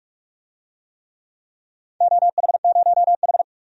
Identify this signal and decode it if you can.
Morse code, heard as a keyed tone, about 45 wpm, 700 Hz. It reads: OH0H
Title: DXCC-100 Call Signs - 4 Characters - 45wpm